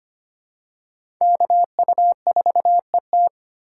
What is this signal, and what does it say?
Morse code, heard as a keyed tone, about 25 wpm, 700 Hz. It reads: KU4ET
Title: USA Call Signs - 5 Characters - 3x Character Spacing - 25wpm